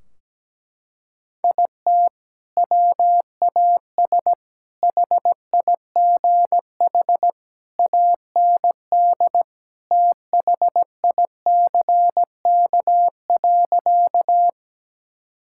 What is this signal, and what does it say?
Morse code, heard as a keyed tone, about 17 wpm, 700 Hz. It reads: IT WAS HIGH AND THICK.
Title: Sentences from Top 1600 Words 17wpm